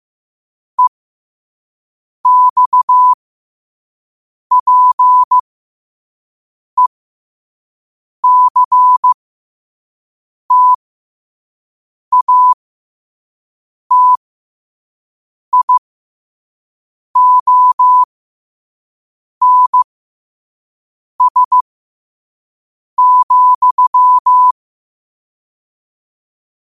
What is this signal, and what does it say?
Morse code, heard as a keyed tone, about 15 wpm, spaced as for 5 wpm, 1000 Hz. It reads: EXPECTATIONS,
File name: qotdfiles / 5wpm